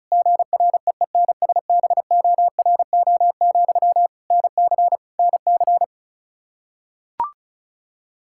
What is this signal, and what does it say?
Morse code, heard as a keyed tone, about 36 wpm, 700 Hz. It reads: GREENSBORO, NC NC
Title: K1USN SST QSO Practice 35wpm